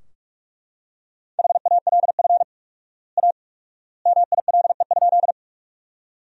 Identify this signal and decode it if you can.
Morse code, heard as a keyed tone, about 41 wpm, 700 Hz. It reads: HALF A MILE?
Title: 4x Word-Spacing - Sentences from Top 400 Words - 45wpm